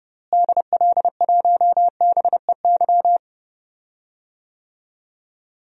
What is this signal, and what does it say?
Morse code, heard as a keyed tone, about 30 wpm, 700 Hz. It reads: DL1BEY
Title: DXCC-100 Call Signs - 6 Characters - Repeated 2x - 30wpm